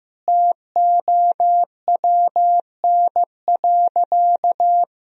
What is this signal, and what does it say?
Morse code, heard as a keyed tone, about 15 wpm, 700 Hz. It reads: TOWN.
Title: Sentences from Top 1000 Words - Review Part 2 - 15wpm